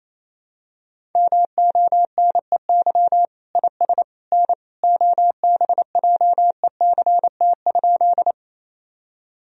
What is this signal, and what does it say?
Morse code, heard as a keyed tone, about 28 wpm, 700 Hz. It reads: MONEY IS N OBJECT?
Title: Sentences from Top 500 Words and Abbreviations 28wpm